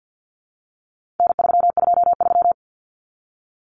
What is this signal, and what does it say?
Morse code, heard as a keyed tone, about 50 wpm, 700 Hz. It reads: N323
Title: Common Cut Numbers - 4-Digit 50wpm